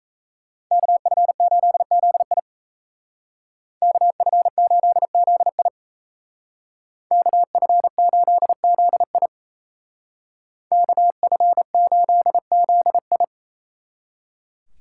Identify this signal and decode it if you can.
Morse code, heard as a keyed tone, about 36 wpm, 700 Hz. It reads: KF8ZI KF8ZI KF8ZI KF8ZI